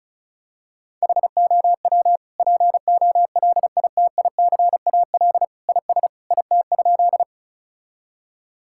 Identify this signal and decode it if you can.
Morse code, heard as a keyed tone, about 36 wpm, 700 Hz. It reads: HOW POLITICAL IS IT?
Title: Sentences from Top 1000 Words - Review Part 5 - 35wpm